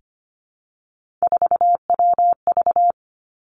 Morse code, heard as a keyed tone, about 25 wpm, 700 Hz. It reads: 4WV
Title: Three Challenging Letter-Number Combinations 25wpm